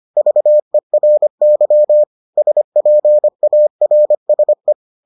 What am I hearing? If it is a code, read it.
Morse code, heard as a keyed tone, about 25 wpm, 600 Hz. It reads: VERY SPARSE